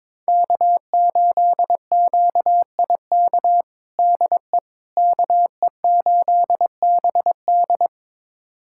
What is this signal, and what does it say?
Morse code, heard as a keyed tone, about 22 wpm, 700 Hz. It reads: K8QIK DE KE8BD
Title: USA Callsign DE Callsign - 5 Characters - 22wpm